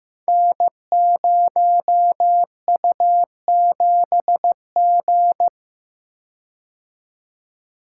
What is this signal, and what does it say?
Morse code, heard as a keyed tone, about 15 wpm, 700 Hz. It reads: N0U7G
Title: Code Groups - 5 Letter:Numbers - 15wpm